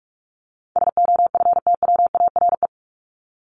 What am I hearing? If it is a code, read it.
Morse code, heard as a keyed tone, about 44 wpm, 700 Hz. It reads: SOFTWARE
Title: Top 501-750 Words 45wpm